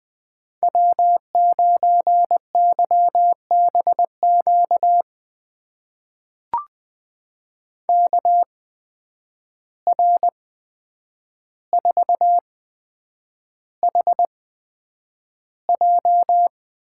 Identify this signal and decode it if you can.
Morse code, heard as a keyed tone, about 20 wpm, spaced as for 9 wpm, 700 Hz. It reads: W9YBQ KR4HJ
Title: USA Call Signs - 5 Characters - 8x Character Spacing - 20wpm